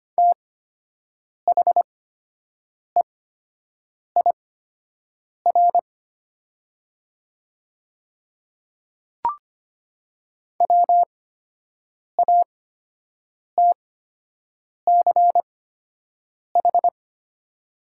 Morse code, heard as a keyed tone, about 25 wpm, spaced as for 7 wpm, 700 Hz. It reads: THEIR WATCH
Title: 8x Character Spacing - Top 300 Words - 25wpm